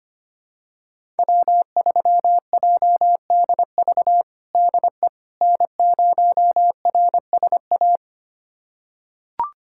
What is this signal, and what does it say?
Morse code, heard as a keyed tone, about 25 wpm, 700 Hz. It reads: W3JDV DE N0RSA
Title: USA Callsign DE Callsign - 5 Characters - 25wpm